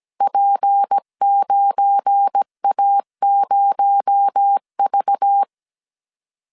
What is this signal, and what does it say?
Morse code, heard as a keyed tone, about 17 wpm, 800 Hz. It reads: P9A0V
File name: CW / DET / 1 9 1500C S C 16wpm 2909